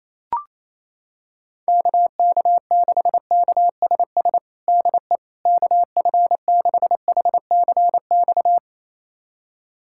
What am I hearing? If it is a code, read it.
Morse code, heard as a keyed tone, about 28 wpm, 700 Hz. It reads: KK6KSS DE KF6HCX
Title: USA Callsign DE Callsign - 6 Characters - 28wpm